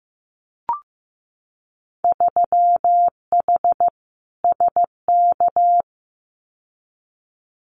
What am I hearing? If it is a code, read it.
Morse code, heard as a keyed tone, about 15 wpm, 700 Hz. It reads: 3H SK
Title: Winter Field Day Exchanges 15wpm